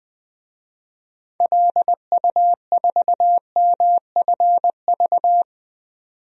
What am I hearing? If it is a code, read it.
Morse code, heard as a keyed tone, about 20 wpm, 700 Hz. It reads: LU4MFV
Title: DXCC-100 Call Signs - 6 Characters - 20wpm